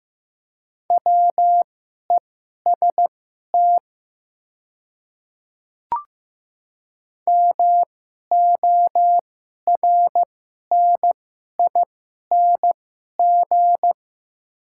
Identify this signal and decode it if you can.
Morse code, heard as a keyed tone, about 15 wpm, spaced as for 11 wpm, 700 Hz. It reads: WEST MORNING
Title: 2x Character Spacing - Top 500 Words - 15wpm